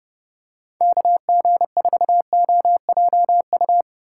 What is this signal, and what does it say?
Morse code, heard as a keyed tone, about 30 wpm, 700 Hz. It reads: KG4OJU
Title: USA Call Signs - 6 Characters - 8x Character Spacing - 30wpm